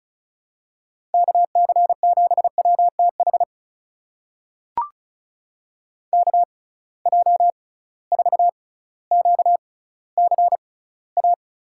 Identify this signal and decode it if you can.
Morse code, heard as a keyed tone, about 36 wpm, spaced as for 18 wpm, 700 Hz. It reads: KC7WTH KJ4QCA
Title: USA Call Signs - 6 Characters - 6x Character Spacing - 35wpm